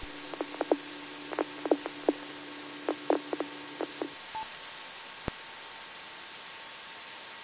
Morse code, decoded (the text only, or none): none